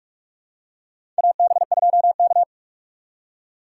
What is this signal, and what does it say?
Morse code, heard as a keyed tone, about 46 wpm, 700 Hz. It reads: AB2K
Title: USA Call Signs - 4 Characters - 3x Character Spacing - 45wpm